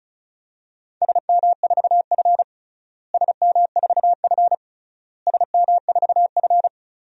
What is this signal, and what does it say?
Morse code, heard as a keyed tone, about 34 wpm, 700 Hz. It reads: SM4F SM4F SM4F
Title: DXCC-100 Call Signs - 4 Characters - Repeated 3x - 35wpm